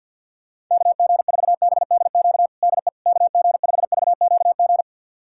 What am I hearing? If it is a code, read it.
Morse code, heard as a keyed tone, about 51 wpm, 700 Hz. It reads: KG4BDQ DE KG5VQG
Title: USA Callsign DE Callsign 50wpm